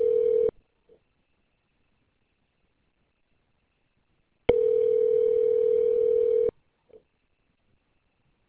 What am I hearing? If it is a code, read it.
Ringback tone (North America), 440 + 480 Hz.